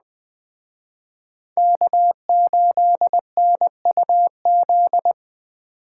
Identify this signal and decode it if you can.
Morse code, heard as a keyed tone, about 21 wpm, 700 Hz. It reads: K8NUZ